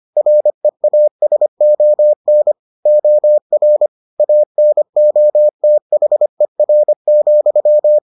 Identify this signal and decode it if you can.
Morse code, heard as a keyed tone, about 25 wpm, 600 Hz. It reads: REASON OR ANOTHER,